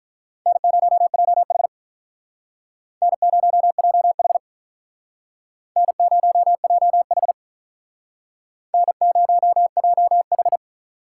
Morse code, heard as a keyed tone, about 43 wpm, 700 Hz. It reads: N0JH N0JH N0JH N0JH
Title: Speed Racing - USA Call Signs 35wpm